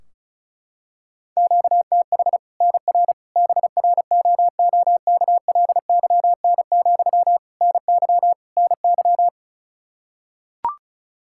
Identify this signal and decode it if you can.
Morse code, heard as a keyed tone, about 36 wpm, 700 Hz. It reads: QTH NR BROOKLYN, NY NY
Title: Practice QSO - Exchange 1 - Beginning - 35wpm